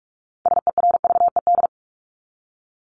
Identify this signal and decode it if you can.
Morse code, heard as a keyed tone, about 44 wpm, 700 Hz. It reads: SERVED
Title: ED Suffix - Top 100 Words with Base Words 45wpm